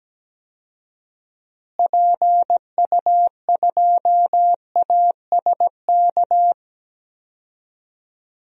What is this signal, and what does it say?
Morse code, heard as a keyed tone, about 17 wpm, 700 Hz. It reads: PU2ASK
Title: DXCC-100 Call Signs - 6 Characters - Repeated 3x - 17wpm